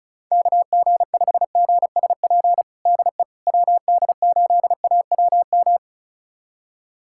Morse code, heard as a keyed tone, about 36 wpm, 700 Hz. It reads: KG5GSP DE WD8AWM
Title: USA Callsign DE Callsign 35wpm